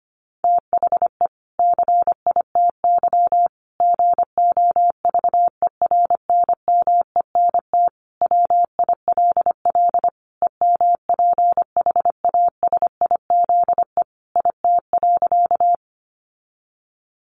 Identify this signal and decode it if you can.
Morse code, heard as a keyed tone, about 25 wpm, 700 Hz. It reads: THE CITY GOVERNMENT WILL EMPHASIZE IT.
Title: Sentences from Top 1500 Words 25wpm